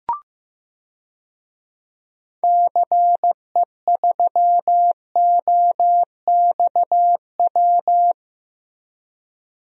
Morse code, heard as a keyed tone, about 16 wpm, 700 Hz. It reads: CE3OXW
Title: DXCC-100 Call Signs - 6 Characters - Repeated 3x - 15wpm